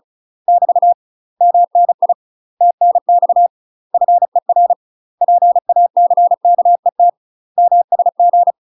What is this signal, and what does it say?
Morse code, heard as a keyed tone, about 34 wpm, 700 Hz. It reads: <BT> MNI TNX FER PACKET MSG
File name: UK MASTER QSOs TRAINING / 35WPM / UK MASTER QSOs 0043